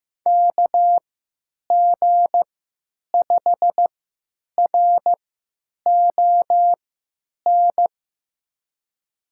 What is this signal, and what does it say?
Morse code, heard as a keyed tone, about 15 wpm, spaced as for 9 wpm, 700 Hz. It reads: KG5RON
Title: USA Call Signs - 3x Character Spacing - 15wpm